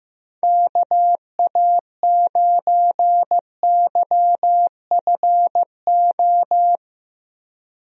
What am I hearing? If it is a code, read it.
Morse code, heard as a keyed tone, about 15 wpm, 700 Hz. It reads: KA9YFO
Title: USA Call Signs - 6 Characters - Repeated 2x - 15wpm